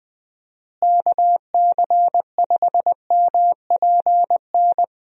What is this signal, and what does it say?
Morse code, heard as a keyed tone, about 21 wpm, 700 Hz. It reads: KC5MPN
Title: USA Call Signs - 6 Characters - Repeated 2x - 20wpm